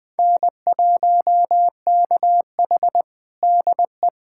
Morse code, heard as a keyed tone, about 20 wpm, 700 Hz. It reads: N1KH DE
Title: USA Callsign DE Callsign - 4 Characters - 20wpm